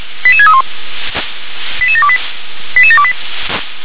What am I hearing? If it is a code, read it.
Five-tone selective call (ZVEI), heard as quick runs of five tones, tones of about 70 ms. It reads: 88631 88728 88728